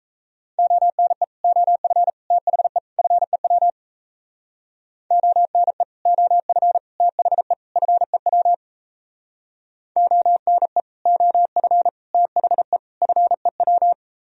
Morse code, heard as a keyed tone, about 38 wpm, 700 Hz. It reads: ONE OF THE FEW ONE OF THE FEW ONE OF THE FEW